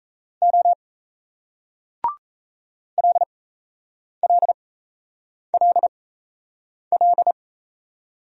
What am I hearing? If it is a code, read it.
Morse code, heard as a keyed tone, about 37 wpm, spaced as for 8 wpm, 700 Hz. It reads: O LLLL